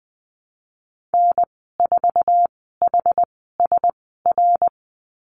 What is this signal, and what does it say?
Morse code, heard as a keyed tone, about 20 wpm, spaced as for 15 wpm, 700 Hz. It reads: N4HSR
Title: 082 Lesson 14g - Number 2 - Call Signs 20wpm